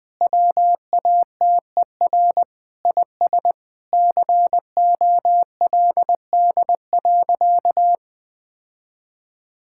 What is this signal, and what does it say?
Morse code, heard as a keyed tone, about 20 wpm, 700 Hz. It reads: WATER IS COLD.
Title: Sentences from Top 1600 Words 20wpm